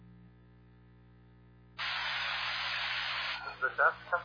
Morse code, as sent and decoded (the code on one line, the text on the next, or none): none